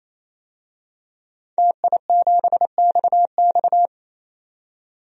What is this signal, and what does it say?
Morse code, heard as a keyed tone, about 28 wpm, 700 Hz. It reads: TI7XX
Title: DXCC-100 Call Signs - 5 Characters - 28wpm